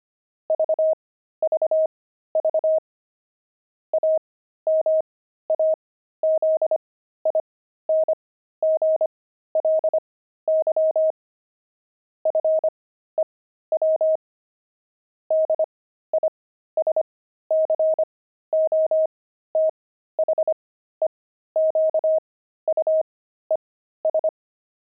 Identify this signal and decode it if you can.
Morse code, heard as a keyed tone, about 25 wpm, spaced as for 13 wpm, 630 Hz. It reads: VVV AMAZINGLY FEW DISCOTHEQUES
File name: pangram-fs-13-cs-25-jukeboxes